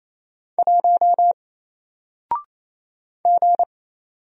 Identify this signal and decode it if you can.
Morse code, heard as a keyed tone, about 28 wpm, 700 Hz. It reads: 1 G
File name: Single-Letter-Number-28wpm